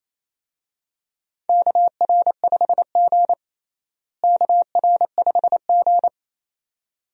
Morse code, heard as a keyed tone, about 28 wpm, 700 Hz. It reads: KR5G KR5G
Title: USA Call Signs - 4 Characters - Repeated 2x - 28wpm